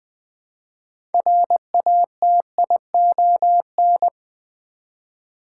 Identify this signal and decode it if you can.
Morse code, heard as a keyed tone, about 21 wpm, 700 Hz. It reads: RATION
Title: Build A Word - 2 Letters At A Time - 8-Letter Words - 20wpm